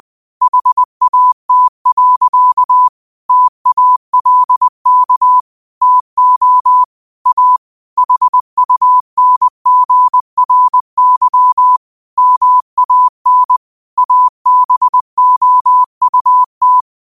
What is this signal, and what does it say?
Morse code, heard as a keyed tone, about 20 wpm, 1000 Hz. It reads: HAT. TALK TO A HUNGRY MAN ABOUT
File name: qotdfiles / 20wpm